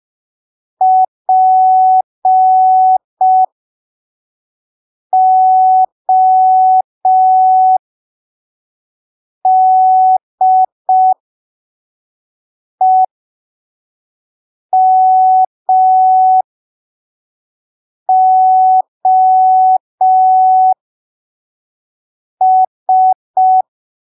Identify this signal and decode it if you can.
Morse code, heard as a keyed tone, about 5 wpm, spaced as for 3 wpm, 750 Hz. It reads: PODEMOS
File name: arquivos / texto / texto 04 5ppm